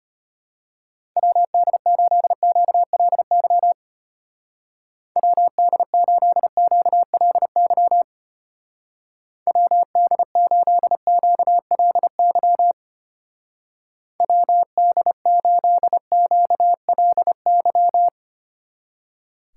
Morse code, heard as a keyed tone, about 33 wpm, 700 Hz. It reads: WD8QLY WD8QLY WD8QLY WD8QLY